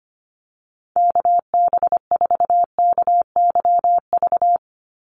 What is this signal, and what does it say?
Morse code, heard as a keyed tone, about 25 wpm, 700 Hz. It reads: KB4KYV
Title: USA Call Signs 25wpm